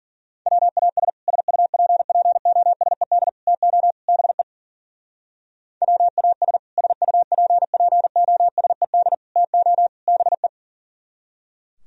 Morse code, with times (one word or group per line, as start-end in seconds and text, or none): 0.5-1.1 WAS
1.3-3.3 SUPPOSED
3.5-3.9 TO
4.1-4.4 BE
5.8-6.6 WAS
6.8-9.1 SUPPOSED
9.4-9.9 TO
10.1-10.5 BE